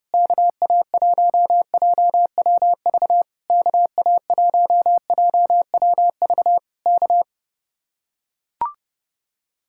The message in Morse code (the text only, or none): KA1JWV KA1JWV K